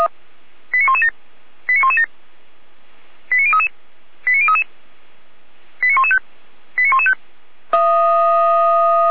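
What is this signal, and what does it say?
Five-tone selective call (ZVEI), heard as quick runs of five tones, tones of about 80 ms. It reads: 89117 89117 89033 89033 89115 89115
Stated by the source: ZVEI five-tone call received on an SDR.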